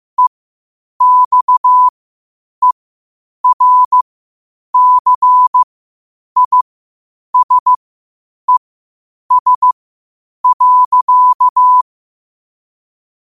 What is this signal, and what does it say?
Morse code, heard as a keyed tone, about 15 wpm, spaced as for 8 wpm, 1000 Hz. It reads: EXERCISES.
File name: qotdfiles / 8wpm